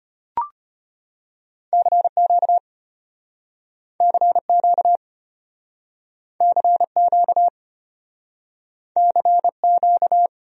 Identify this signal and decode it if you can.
Morse code, heard as a keyed tone, about 34 wpm, 700 Hz. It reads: CQ CQ CQ CQ